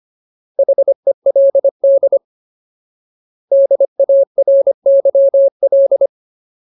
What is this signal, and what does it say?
Morse code, heard as a keyed tone, about 25 wpm, 550 Hz. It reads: HELD DARYL
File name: Ultimate Word List / 25WPM Ultimate